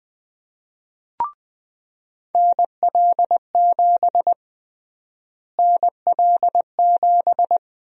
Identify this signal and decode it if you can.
Morse code, heard as a keyed tone, about 21 wpm, 700 Hz. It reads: NL7 NL7
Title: USA Call Sign Prefixes - Repeated 2x - 20wpm